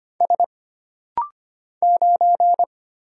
Morse code, heard as a keyed tone, about 25 wpm, 700 Hz. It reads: S 9
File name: Single Character (Letter:Number:Punctuation:Prosign) - Rapid-Fire - 25wpm